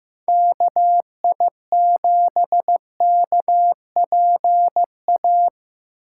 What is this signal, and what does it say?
Morse code, heard as a keyed tone, about 15 wpm, 700 Hz. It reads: KI7KPA
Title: USA Call Signs - 6 Characters - 8x Character Spacing - 15wpm